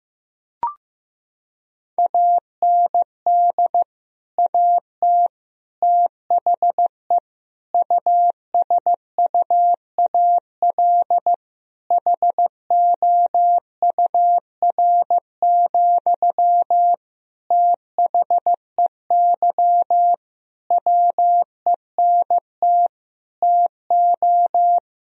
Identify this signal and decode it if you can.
Morse code, heard as a keyed tone, about 15 wpm, 700 Hz. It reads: AND AT THE USUAL HOUR, THEY WENT TO
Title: Sentences from Top 1000 Words - Review Part 2 - 15wpm